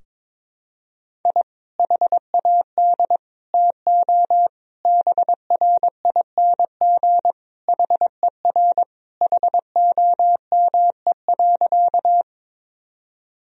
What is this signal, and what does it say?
Morse code, heard as a keyed tone, about 22 wpm, 700 Hz. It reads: I HAD TO BRING HER HOME.
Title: Sentences from Top 1000 Words - Review Part 3 - 22wpm